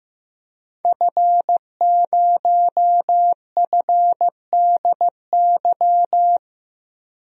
Morse code, heard as a keyed tone, about 15 wpm, 700 Hz. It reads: F0FDY
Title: DXCC-100 Call Signs - 5 Characters - 15wpm